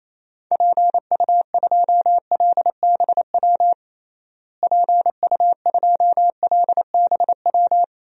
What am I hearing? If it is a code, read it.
Morse code, heard as a keyed tone, about 28 wpm, 700 Hz. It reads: PU2LBW PU2LBW
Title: DXCC-100 Call Signs - 6 Characters - Repeated 3x - 28wpm